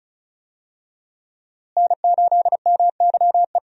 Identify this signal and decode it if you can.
Morse code, heard as a keyed tone, about 34 wpm, 700 Hz. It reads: N8MYE
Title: USA Call Signs - 5 Characters - Repeated 2x - 35wpm